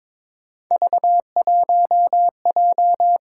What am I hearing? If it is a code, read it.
Morse code, heard as a keyed tone, about 28 wpm, 700 Hz. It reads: V1J